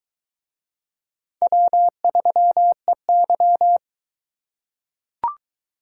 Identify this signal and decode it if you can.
Morse code, heard as a keyed tone, about 22 wpm, 700 Hz. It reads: W3EY